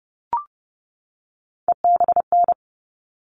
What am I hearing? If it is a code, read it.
Morse code, heard as a keyed tone, about 30 wpm, 700 Hz. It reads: EBN
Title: All Cut Numbers - 3-Digit 30wpm